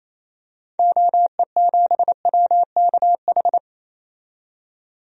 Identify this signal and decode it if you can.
Morse code, heard as a keyed tone, about 28 wpm, 700 Hz. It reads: OE7WKH
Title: DXCC-100 Call Signs - 6 Characters - 28wpm